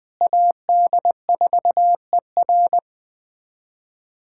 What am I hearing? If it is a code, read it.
Morse code, heard as a keyed tone, about 20 wpm, 700 Hz. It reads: AD4ER
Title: USA Callsign DE Callsign - 5 Characters - 20wpm